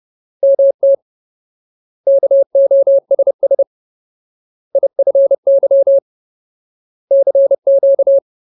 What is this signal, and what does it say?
Morse code, heard as a keyed tone, about 30 wpm, 550 Hz. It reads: MT KOSS IFY CQ
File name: Ultimate Word List / 30WPM Ultimate